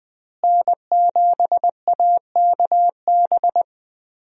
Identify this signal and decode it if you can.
Morse code, heard as a keyed tone, about 20 wpm, 700 Hz. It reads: N7AKB
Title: USA Call Signs - 5 Characters - Repeated 2x - 20wpm